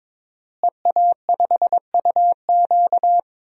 Morse code, heard as a keyed tone, about 21 wpm, 700 Hz. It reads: EA5UQ
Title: DXCC-100 Call Signs - 5 Characters - Repeated 2x - 22wpm